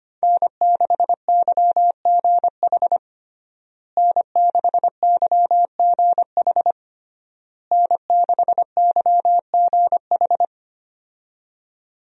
Morse code, heard as a keyed tone, about 25 wpm, 700 Hz. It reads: N6YGH N6YGH N6YGH
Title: USA Call Signs - 5 Characters - Repeated 3x - 25wpm